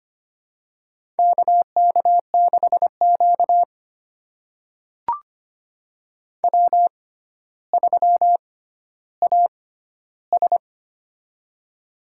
Morse code, heard as a keyed tone, about 25 wpm, spaced as for 13 wpm, 700 Hz. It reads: KK6Q W3AS
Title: USA Call Signs - 4 Characters - 6x Character Spacing - 25wpm